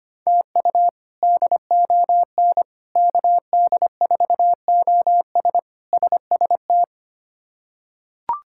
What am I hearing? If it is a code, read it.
Morse code, heard as a keyed tone, about 25 wpm, 700 Hz. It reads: TU DON KD4OS SST
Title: K1USN SST QSO Practice - 25wpm